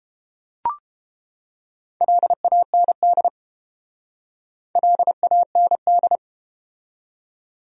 Morse code, heard as a keyed tone, about 33 wpm, 700 Hz. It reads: LAND LAND